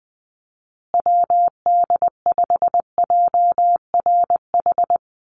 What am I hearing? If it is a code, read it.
Morse code, heard as a keyed tone, about 20 wpm, 700 Hz. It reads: WD5JRH
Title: USA Call Signs Repeated 2x 20wpm